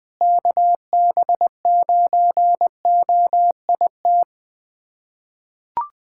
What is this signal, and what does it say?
Morse code, heard as a keyed tone, about 20 wpm, 700 Hz. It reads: KB9OIT
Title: USA Callsign DE Callsign 20wpm